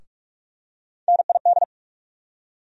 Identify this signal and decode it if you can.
Morse code, heard as a keyed tone, about 43 wpm, 700 Hz. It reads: NID
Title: USA Call Sign Postfixes - 3 Letters 45wpm